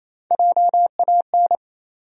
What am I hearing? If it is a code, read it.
Morse code, heard as a keyed tone, about 28 wpm, 700 Hz. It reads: JAN
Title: USA Common Names - Max 3 Letters - 28wpm